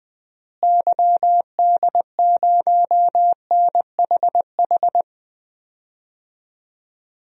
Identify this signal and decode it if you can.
Morse code, heard as a keyed tone, about 20 wpm, 700 Hz. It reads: YD0NHH
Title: DXCC-100 Call Signs - 6 Characters - Repeated 2x - 20wpm